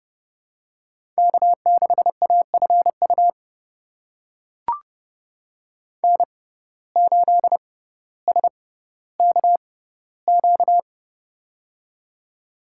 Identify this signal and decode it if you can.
Morse code, heard as a keyed tone, about 32 wpm, spaced as for 16 wpm, 700 Hz. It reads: K6AFU N8SKQ
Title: USA Call Signs - 5 Characters - 6x Character Spacing - 30wpm